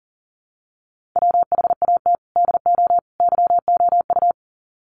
Morse code, heard as a keyed tone, about 40 wpm, 700 Hz. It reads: WHAT DO YOU
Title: Sets of 3 Words 40wpm